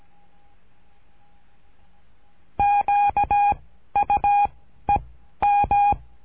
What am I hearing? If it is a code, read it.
Morse code, heard as a keyed tone, about 17 wpm, 820 Hz. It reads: Q U E M
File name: cursos / CURSO DE TELEGRAFIA / Telegrafia - Trilha 13